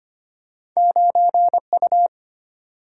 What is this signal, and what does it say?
Morse code, heard as a keyed tone, about 25 wpm, 700 Hz. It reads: 9U
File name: Two Letter-Number Combinations - 25wpm